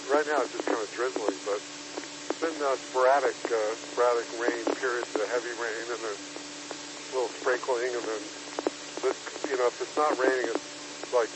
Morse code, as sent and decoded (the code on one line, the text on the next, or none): none